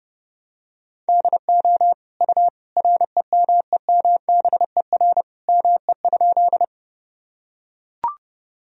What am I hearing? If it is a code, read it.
Morse code, heard as a keyed tone, about 32 wpm, 700 Hz. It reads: DO U REMEMBER ME?
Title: Sentences from Top 400 Words and Abbreviations 30wpm